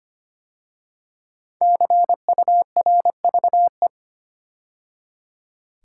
Morse code, heard as a keyed tone, about 25 wpm, 700 Hz. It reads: CURVE
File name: 5 Letter Words - First 3 Letters Given - 25wpm